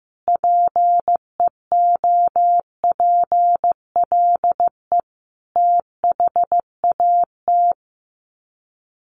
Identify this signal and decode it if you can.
Morse code, heard as a keyed tone, about 15 wpm, 700 Hz. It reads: PEOPLE THAT
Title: Sets of 5 Words Encore2 15wpm